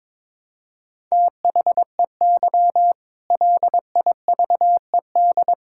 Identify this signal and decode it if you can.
Morse code, heard as a keyed tone, about 22 wpm, 700 Hz. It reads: THEY LIVED
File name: practice / 22wpm / Sentences with 8-Word Phrases - 22wpm